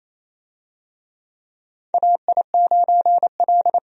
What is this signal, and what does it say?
Morse code, heard as a keyed tone, about 28 wpm, 700 Hz. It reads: AI9L
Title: USA Call Signs - 4 Characters - 28wpm